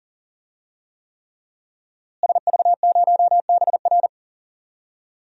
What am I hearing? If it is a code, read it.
Morse code, heard as a keyed tone, about 40 wpm, 700 Hz. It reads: SV0BR
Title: DXCC-100 Call Signs - 5 Characters - Repeated 3x - 40wpm